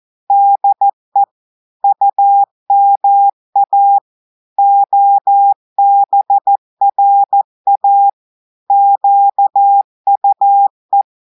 Morse code, heard as a keyed tone, about 14 wpm, 800 Hz. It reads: DE UMA OBRA QUE
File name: audios / portugais / portugais texte en clair 1 vitesse 14 mots minutes